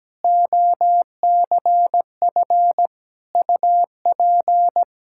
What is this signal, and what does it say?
Morse code, heard as a keyed tone, about 17 wpm, 700 Hz. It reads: OCF UP
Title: Practice QSO - Exchange 2 - Info - 17wpm